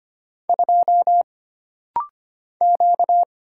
Morse code, heard as a keyed tone, about 25 wpm, 700 Hz. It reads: 2 Q
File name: Single Character (Letter:Number:Punctuation:Prosign) - Rapid-Fire - 25wpm